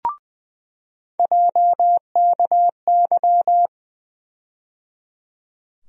Morse code, heard as a keyed tone, about 20 wpm, 700 Hz. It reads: JKY
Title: USA Call Sign Postfixes - 3 Letters 20wpm